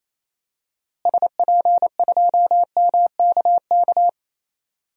Morse code, heard as a keyed tone, about 28 wpm, 700 Hz. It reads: SP2MKK
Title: DXCC-100 Call Signs - 6 Characters - Repeated 3x - 28wpm